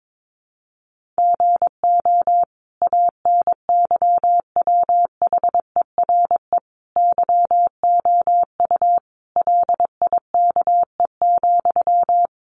Morse code, heard as a keyed tone, about 22 wpm, 700 Hz. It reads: GO ANYWHERE YOU LIKE,
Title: Sentences from Top 1400 Words 22wpm